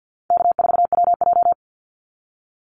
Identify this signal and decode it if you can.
Morse code, heard as a keyed tone, about 50 wpm, 700 Hz. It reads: K4WJ
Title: Call Signs - 50wpm